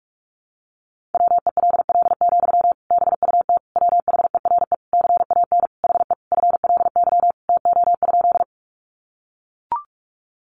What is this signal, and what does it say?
Morse code, heard as a keyed tone, about 44 wpm, 700 Hz. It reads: WELL, BUT WHERE CAN HE FLY TO?
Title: Sentences from Top 500 Words 45wpm